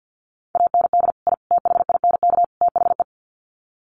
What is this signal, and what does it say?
Morse code, heard as a keyed tone, about 50 wpm, 700 Hz. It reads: AND I THINK THE
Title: Sets of 4 Words 50wpm